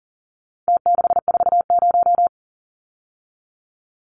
Morse code, heard as a keyed tone, about 40 wpm, 700 Hz. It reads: T640
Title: Numbers - Contest Style - 4-Digit 40wpm